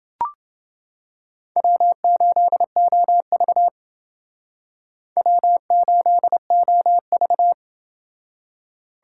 Morse code, heard as a keyed tone, about 28 wpm, 700 Hz. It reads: W8OV W8OV